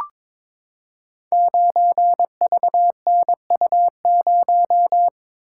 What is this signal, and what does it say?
Morse code, heard as a keyed tone, about 22 wpm, 700 Hz. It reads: 9VNU0
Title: Code Groups - 5 Letter:Numbers - 22wpm